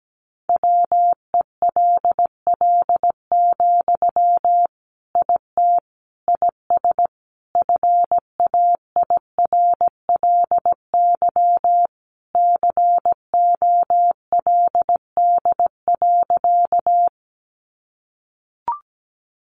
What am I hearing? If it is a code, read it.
Morse code, heard as a keyed tone, about 17 wpm, 700 Hz. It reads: WELL, IT IS FAIRLY COLD.
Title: Sentences from Top 1100 Words 17wpm